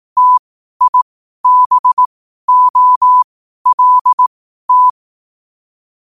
Morse code, heard as a keyed tone, about 18 wpm, spaced as for 12 wpm, 1000 Hz. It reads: TIBOLT
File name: qotdfiles / 12wpm